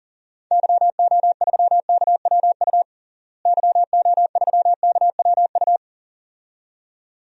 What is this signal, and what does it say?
Morse code, heard as a keyed tone, about 37 wpm, 700 Hz. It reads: YO3KWU YO3KWU
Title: DXCC-100 Call Signs - 6 Characters - Repeated 2x - 40wpm